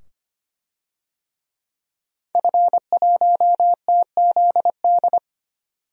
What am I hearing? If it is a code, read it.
Morse code, heard as a keyed tone, about 25 wpm, 700 Hz. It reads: F1TZD
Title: DXCC-100 Call Signs - 5 Characters - 25wpm